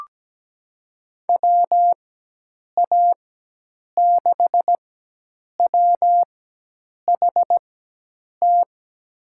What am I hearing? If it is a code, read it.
Morse code, heard as a keyed tone, about 17 wpm, spaced as for 8 wpm, 700 Hz. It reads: WA6WHT
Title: USA Call Signs - 6 Characters - 4x Character Spacing - 17wpm